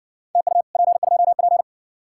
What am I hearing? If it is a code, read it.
Morse code, heard as a keyed tone, about 60 wpm, 700 Hz. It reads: TU P2P